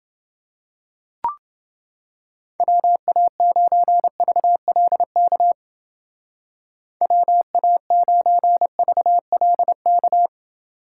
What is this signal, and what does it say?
Morse code, heard as a keyed tone, about 28 wpm, 700 Hz. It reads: WA9VLK WA9VLK